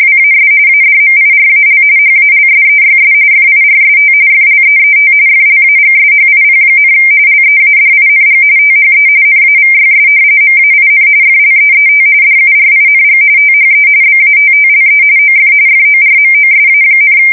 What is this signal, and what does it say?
RTTY, about 45.45 baud. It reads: WAS ADMIRAL HALSEYS FIRST FLEET FLAGSHIP, LEADING THE NAVYS MAIN FLEET DURING THE BATTLE OF FORMOSA,
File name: fsk170 / battleshipnj